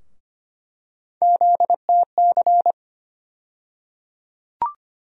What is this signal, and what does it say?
Morse code, heard as a keyed tone, about 25 wpm, 700 Hz. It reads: ZTC
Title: USA Call Sign Postfixes - 3 Letters 25wpm